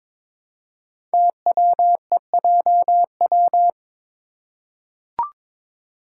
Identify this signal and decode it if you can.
Morse code, heard as a keyed tone, about 22 wpm, 700 Hz. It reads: TWEJW
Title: Code Groups - 5 Letters - 22wpm